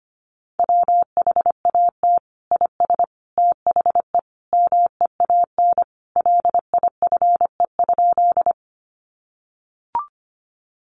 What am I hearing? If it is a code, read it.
Morse code, heard as a keyed tone, about 25 wpm, 700 Hz. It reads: WHAT IS THE MEAN LIFE?
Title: Sentences from Top 900 Words 25wpm